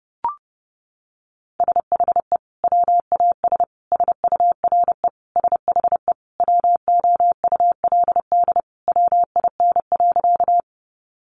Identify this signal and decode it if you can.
Morse code, heard as a keyed tone, about 30 wpm, 700 Hz. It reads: SHE WAS SURE SHE WOULD WIN.
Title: Sentences from Top 1500 Words 30wpm